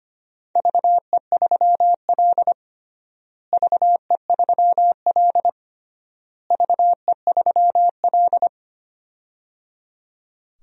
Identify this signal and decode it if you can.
Morse code, heard as a keyed tone, about 25 wpm, 700 Hz. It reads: VE3L VE3L VE3L
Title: DXCC-100 Call Signs - 4 Characters - Repeated 3x - 25wpm